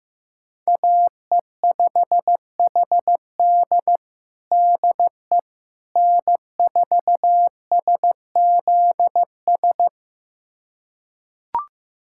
Morse code, heard as a keyed tone, about 16 wpm, 700 Hz. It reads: AE5HD DE N4SZS
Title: USA Callsign DE Callsign - 5 Characters - 15wpm